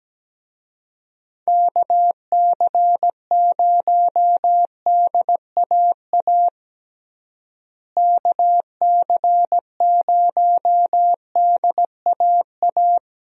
Morse code, heard as a keyed tone, about 17 wpm, 700 Hz. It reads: KC0DAA KC0DAA
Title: USA Call Signs - 6 Characters - Repeated 3x - 17wpm